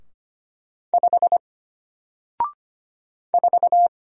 Morse code, heard as a keyed tone, about 25 wpm, 700 Hz. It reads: 5 4
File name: Single-Letter-Number-25wpm